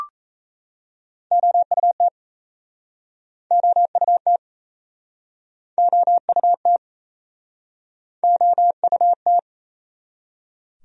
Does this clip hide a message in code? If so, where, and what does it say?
Yes: Morse code from 1.3 to 9.4 s, heard as a keyed tone, about 38 wpm, 700 Hz. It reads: OUT OUT OUT OUT